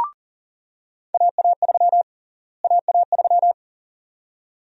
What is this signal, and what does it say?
Morse code, heard as a keyed tone, about 38 wpm, 700 Hz. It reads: AA3 AA3
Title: USA Call Sign Prefixes - Repeated 2x - 40wpm